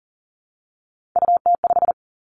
Morse code, heard as a keyed tone, about 39 wpm, 700 Hz. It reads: UT5
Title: Callsign Prefixes for Top 25 DXCC - 40wpm